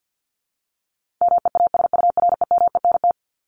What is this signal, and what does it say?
Morse code, heard as a keyed tone, about 50 wpm, 700 Hz. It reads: MEASUREMENT
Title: Top 1751-2000 Words 50wpm